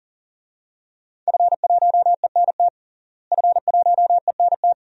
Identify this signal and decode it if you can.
Morse code, heard as a keyed tone, about 40 wpm, 700 Hz. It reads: F1ENT F1ENT
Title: DXCC-100 Call Signs - 5 Characters - Repeated 2x - 40wpm